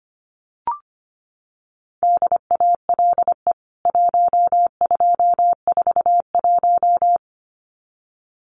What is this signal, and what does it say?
Morse code, heard as a keyed tone, about 25 wpm, 700 Hz. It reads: DALE 1241
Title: CWT Exchange 25wpm